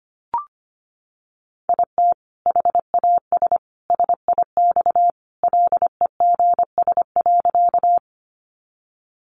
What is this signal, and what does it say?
Morse code, heard as a keyed tone, about 25 wpm, 700 Hz. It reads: IT HAS SIX LEGS.
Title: Additional Easy Phrases 25wpm